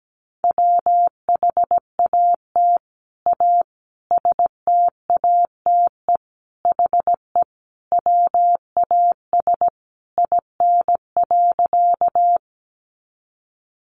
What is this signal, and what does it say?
Morse code, heard as a keyed tone, about 17 wpm, 700 Hz. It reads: WHAT A STATE HE WAS IN.
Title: Sentences from Top 400 Words 17wpm